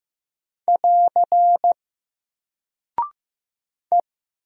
Morse code, heard as a keyed tone, about 15 wpm, 700 Hz. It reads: <AR> E
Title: Single Character (Letter:Number:Punctuation:Prosign) 15wpm